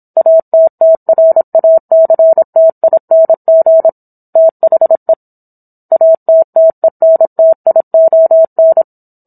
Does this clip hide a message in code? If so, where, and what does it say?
Yes: Morse code from 0.2 to 8.8 s, heard as a keyed tone, about 27 wpm, 650 Hz. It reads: ATTRACTING THE ATTENTION